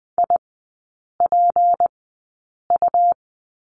Morse code, heard as a keyed tone, about 20 wpm, spaced as for 8 wpm, 700 Hz. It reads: IPU
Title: 152 Lesson 25e - Letter p - Three Characters 20wpm